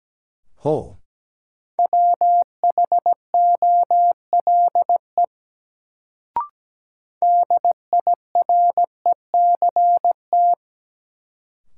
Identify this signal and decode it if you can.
Morse code, heard as a keyed tone, about 17 wpm, 700 Hz. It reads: WHOLE DIRECT
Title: Sentences from Top 400 Words - New Word Drill - 17wpm